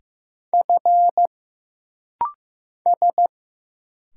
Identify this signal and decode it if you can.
Morse code, heard as a keyed tone, about 15 wpm, 700 Hz. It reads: F S
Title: Single Letter-Number Alt - Rapid-Fire 15wpm